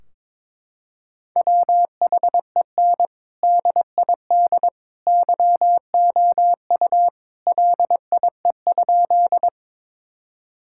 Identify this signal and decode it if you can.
Morse code, heard as a keyed tone, about 22 wpm, 700 Hz. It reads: WHEN DID YOU LIE?
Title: Sentences from Top 1600 Words 22wpm